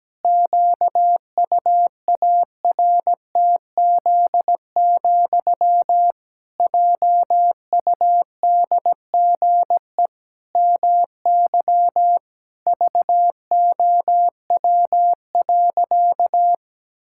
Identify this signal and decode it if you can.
Morse code, heard as a keyed tone, about 17 wpm, 700 Hz. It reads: QUARTZ, JUDGE MY VOW.
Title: Pangram Challenge v2 - 17wpm